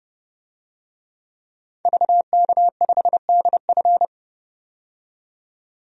Morse code, heard as a keyed tone, about 30 wpm, 690 Hz. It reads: VK5DF
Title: DXCC-100 Call Signs - 5 Characters - 30wpm